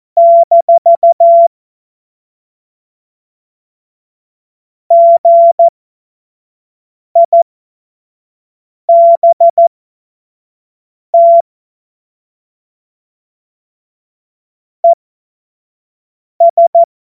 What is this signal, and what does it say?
Morse code, heard as a keyed tone, about 14 wpm, spaced as for 5 wpm, 680 Hz. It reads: - GIBT ES
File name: Witz-09-September-2025-25-F70